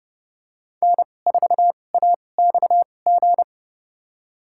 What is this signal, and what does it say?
Morse code, heard as a keyed tone, about 29 wpm, spaced as for 21 wpm, 700 Hz. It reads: N4AXG
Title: USA Call Signs - 5 Characters - 2x Character Spacing - 30wpm